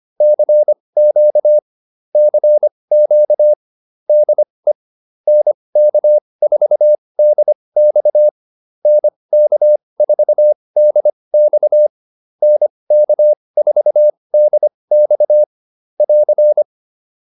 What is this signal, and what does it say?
Morse code, heard as a keyed tone, about 25 wpm, spaced as for 20 wpm, 600 Hz. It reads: CQ CQ DE NK4DX NK4DX NK4DX <AR>